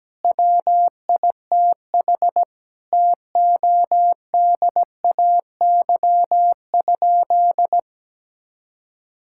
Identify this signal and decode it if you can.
Morse code, heard as a keyed tone, about 17 wpm, 700 Hz. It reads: WITH TODAY?
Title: Sentences from Top 1000 Words - Review Part 6 - 17wpm